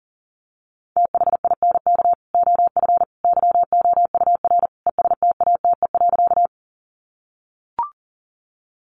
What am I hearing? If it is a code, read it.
Morse code, heard as a keyed tone, about 40 wpm, 700 Hz. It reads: THINK OF YOUR ESTATE.
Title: Sentences from Top 1200 Words 40wpm